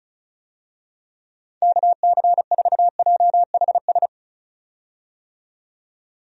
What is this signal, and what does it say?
Morse code, heard as a keyed tone, about 36 wpm, 700 Hz. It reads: KC4JHS
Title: USA Call Signs - 6 Characters - 35wpm